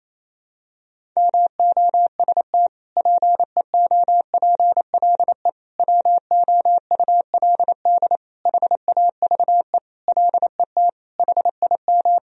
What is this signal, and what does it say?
Morse code, heard as a keyed tone, about 28 wpm, 700 Hz. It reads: MOST PEOPLE WOULD HAVE LET HIM
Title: Sentences from Top 1000 Words - Review Part 1 - 28wpm